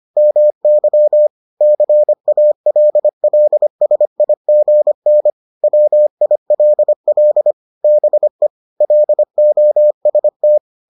Morse code, heard as a keyed tone, about 25 wpm, 600 Hz. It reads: MY CALLSIGN WILL BE LOST